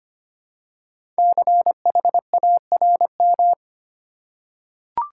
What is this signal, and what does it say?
Morse code, heard as a keyed tone, about 25 wpm, 700 Hz. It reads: CHARM
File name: 5 Letter Words - First 3 Letters Given - 25wpm